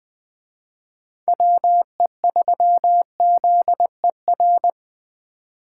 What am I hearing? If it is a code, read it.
Morse code, heard as a keyed tone, about 21 wpm, 700 Hz. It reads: WE3ZER
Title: Speed Racing - USA 2x3 Call Signs 17wpm